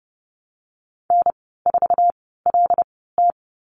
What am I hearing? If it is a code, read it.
Morse code, heard as a keyed tone, about 30 wpm, spaced as for 17 wpm, 700 Hz. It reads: N4LT
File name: Lesson 14g - Number 2 - Call Signs 30wpm